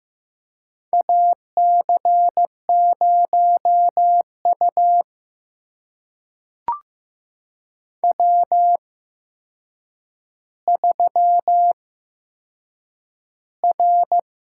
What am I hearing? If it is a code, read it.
Morse code, heard as a keyed tone, about 15 wpm, spaced as for 7 wpm, 700 Hz. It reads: AC0U W3R
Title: USA Call Signs - 8x Character Spacing - 15wpm